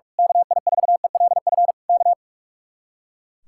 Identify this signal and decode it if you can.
Morse code, heard as a keyed tone, about 45 wpm, 700 Hz. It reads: KI4ELF K
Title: Speed-Racing Common QSO Phrases - 45wpm